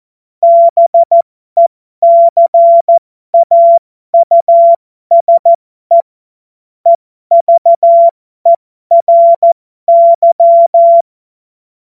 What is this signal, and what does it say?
Morse code, heard as a keyed tone, about 14 wpm, spaced as for 12 wpm, 680 Hz. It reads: BECAUSE EVERY
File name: Morsekurs / Witze